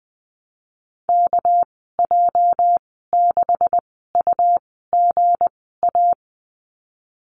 Morse code, heard as a keyed tone, about 20 wpm, spaced as for 15 wpm, 700 Hz. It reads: KJ6UGA
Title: 219 Lesson 35g - Number 0 - Call Signs 20wpm